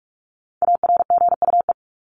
Morse code, heard as a keyed tone, about 44 wpm, 700 Hz. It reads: ARGUE
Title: Top 1501-1750 Words 45wpm